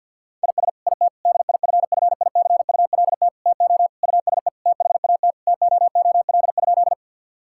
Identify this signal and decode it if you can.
Morse code, heard as a keyed tone, about 50 wpm, 700 Hz. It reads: IS IT DIFFICULT TO USE THAT TOOL?
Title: Sentences from Top 1000 Words - Review Part 6 - 50wpm